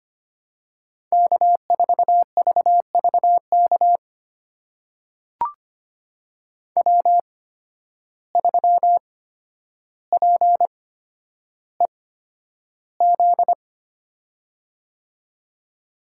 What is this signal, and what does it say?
Morse code, heard as a keyed tone, about 25 wpm, spaced as for 11 wpm, 700 Hz. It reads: K4VVK W3PEZ
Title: USA Call Signs - 5 Characters - 8x Character Spacing - 25wpm